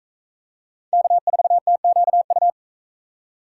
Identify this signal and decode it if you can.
Morse code, heard as a keyed tone, about 40 wpm, 700 Hz. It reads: K4TQU